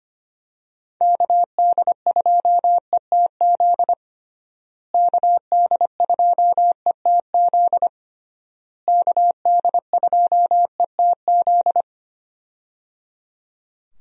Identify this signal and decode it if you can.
Morse code, heard as a keyed tone, about 25 wpm, 700 Hz. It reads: KD2ETZ KD2ETZ KD2ETZ
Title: USA Call Signs - 6 Characters - Repeated 3x - 25wpm